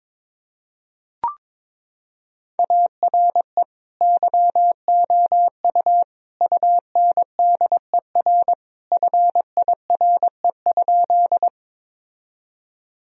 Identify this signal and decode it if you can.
Morse code, heard as a keyed tone, about 22 wpm, 700 Hz. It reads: ARE YOU UNDER FIRE?
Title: Sentences from Top 1000 Words - Review Part 1 - 22wpm